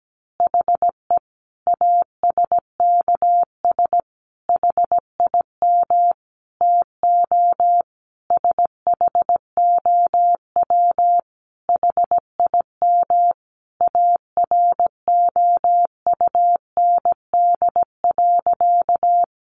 Morse code, heard as a keyed tone, about 17 wpm, 700 Hz. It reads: HE ASKS HIM TO SHOW HIM AROUND.
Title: Easy Phrases 17wpm